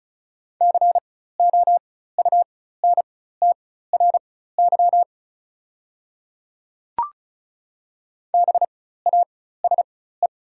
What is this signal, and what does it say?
Morse code, heard as a keyed tone, about 34 wpm, spaced as for 16 wpm, 700 Hz. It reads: COUNTRY BASE
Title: 4x Character Spacing - Top 300 Words - 35wpm